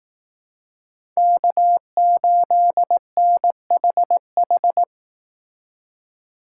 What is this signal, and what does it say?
Morse code, heard as a keyed tone, about 18 wpm, 700 Hz. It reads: K8NHH
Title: Speed Racing - USA 1x3 Call Signs 15wpm